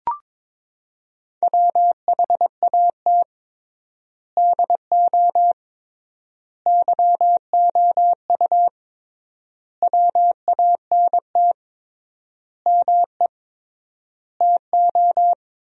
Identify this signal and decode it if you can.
Morse code, heard as a keyed tone, about 22 wpm, 700 Hz. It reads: WHAT DO YOU WANT ME TO
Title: 3x Word-Spacing - Sentences from Top 200 Words - 22wpm